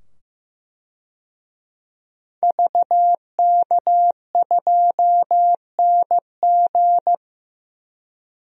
Morse code, heard as a keyed tone, about 15 wpm, 700 Hz. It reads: VK2NG
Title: DXCC-100 Call Signs - 5 Characters - 15wpm